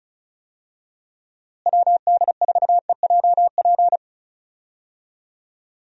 Morse code, heard as a keyed tone, about 36 wpm, 700 Hz. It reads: WD4EJP
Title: USA Call Signs - 6 Characters - Repeated 3x - 35wpm